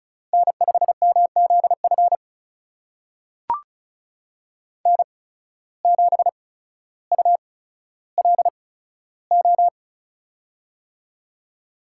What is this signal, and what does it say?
Morse code, heard as a keyed tone, about 36 wpm, spaced as for 15 wpm, 700 Hz. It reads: N5MZF N7ULO
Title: USA Call Signs - 5 Characters - 8x Character Spacing - 35wpm